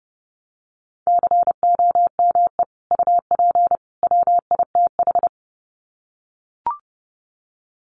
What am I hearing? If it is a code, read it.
Morse code, heard as a keyed tone, about 30 wpm, 700 Hz. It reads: COME UP WITH
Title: Sets of 3 Words 30wpm